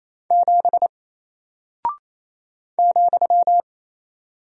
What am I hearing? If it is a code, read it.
Morse code, heard as a keyed tone, about 28 wpm, 700 Hz. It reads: 7 ,
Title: Single Character (Letter:Number:Punctuation:Prosign) Alt 28wpm